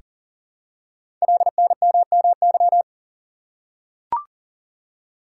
Morse code, heard as a keyed tone, about 42 wpm, 700 Hz. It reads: LNMMY
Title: Code Groups - 5 Letters - 40wpm